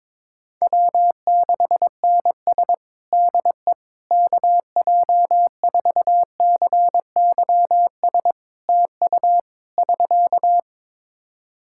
Morse code, heard as a keyed tone, about 22 wpm, 700 Hz. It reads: W6NS DE KJ4CYS TU <SK>
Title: Practice QSO - Exchange 4 - Ending - 22wpm